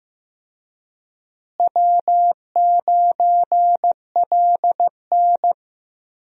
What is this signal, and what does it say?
Morse code, heard as a keyed tone, about 15 wpm, 700 Hz. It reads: W9LN
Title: USA Call Signs - 4 Characters - Repeated 3x - 15wpm